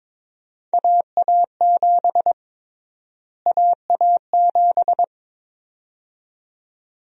Morse code, heard as a keyed tone, about 22 wpm, 700 Hz. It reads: AA7 AA7
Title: USA Call Sign Prefixes - Repeated 2x - 22wpm